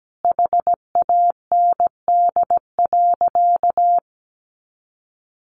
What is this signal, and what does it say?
Morse code, heard as a keyed tone, about 17 wpm, 700 Hz. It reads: HAND.
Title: Sentences from Top 200 Words 17wpm